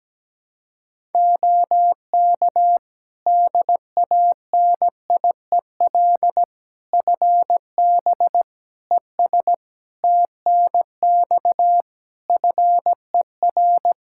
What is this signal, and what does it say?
Morse code, heard as a keyed tone, about 17 wpm, 700 Hz. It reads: OK DANIEL FB ES TNX FER
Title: K1USN SST QSO Practice 17wpm